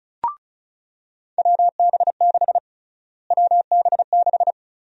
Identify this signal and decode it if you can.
Morse code, heard as a keyed tone, about 34 wpm, 700 Hz. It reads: WB6 WB6
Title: USA Call Sign Prefixes - Repeated 2x - 35wpm